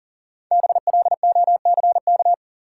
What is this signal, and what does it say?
Morse code, heard as a keyed tone, about 39 wpm, 700 Hz. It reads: BLOCK
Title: Sentences from Top 1600 Words - New Word Drill - 40wpm